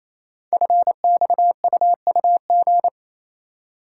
Morse code, heard as a keyed tone, about 27 wpm, 700 Hz. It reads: FXUUG
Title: Code Groups - 5 Letters - 28wpm